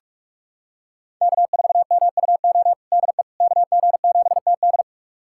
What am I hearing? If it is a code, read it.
Morse code, heard as a keyed tone, about 46 wpm, 700 Hz. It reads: K4MUO DE KG7TD
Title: USA Callsign DE Callsign - 5 Characters - 45wpm